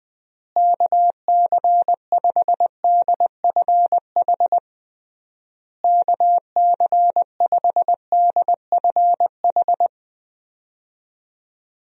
Morse code, heard as a keyed tone, about 20 wpm, 700 Hz. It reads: KC5DFH KC5DFH
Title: USA Call Signs - 6 Characters - Repeated 3x - 20wpm